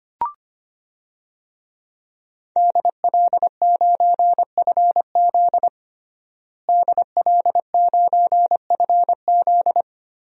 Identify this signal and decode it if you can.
Morse code, heard as a keyed tone, about 25 wpm, 700 Hz. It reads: DL9FZ DL9FZ
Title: DXCC-100 Call Signs - 5 Characters - Repeated 3x - 25wpm